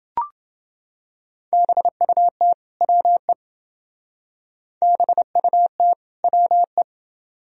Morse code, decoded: BUT WE BUT WE

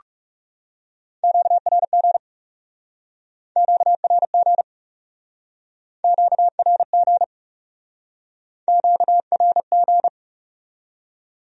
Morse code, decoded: QRG QRG QRG QRG

